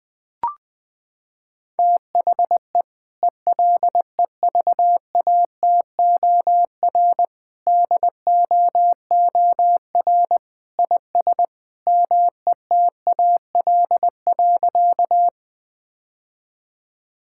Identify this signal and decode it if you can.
Morse code, heard as a keyed tone, about 20 wpm, 700 Hz. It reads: THE ELEVATOR DOOR IS METAL.
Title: Sentences from Top 1600 Words 20wpm